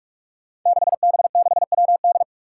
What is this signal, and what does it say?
Morse code, heard as a keyed tone, about 46 wpm, 700 Hz. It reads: 6B6WD
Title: Five Challenging Letter-Number Combinations 45wpm